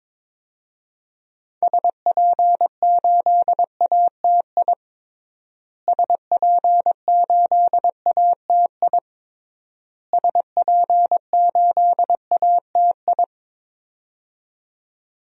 Morse code, heard as a keyed tone, about 21 wpm, 700 Hz. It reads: SP8ATI SP8ATI SP8ATI
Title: DXCC-100 Call Signs - 6 Characters - Repeated 3x - 22wpm